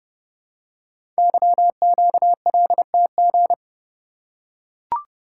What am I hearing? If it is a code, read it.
Morse code, heard as a keyed tone, about 30 wpm, 700 Hz. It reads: YQLTG
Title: Code Groups - 5 Letters - 30wpm